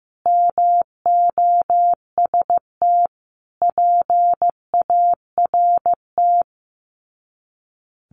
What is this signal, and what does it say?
Morse code, heard as a keyed tone, about 15 wpm, 700 Hz. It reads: MOST PART
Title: Sets of 5 Words Encore 15wpm